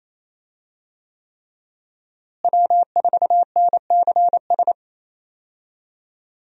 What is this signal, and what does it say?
Morse code, heard as a keyed tone, about 28 wpm, 700 Hz. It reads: W4NCS
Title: USA Call Signs - 5 Characters - Repeated 3x - 28wpm